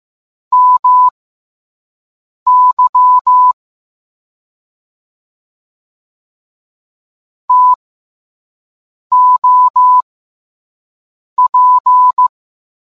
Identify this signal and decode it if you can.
Morse code, heard as a keyed tone, about 15 wpm, spaced as for 5 wpm, 1000 Hz. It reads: MY TOP